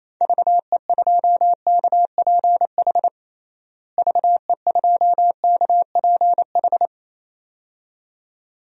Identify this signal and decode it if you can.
Morse code, heard as a keyed tone, about 28 wpm, 700 Hz. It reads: VE2KPH VE2KPH
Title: DXCC-100 Call Signs - 6 Characters - Repeated 2x - 28wpm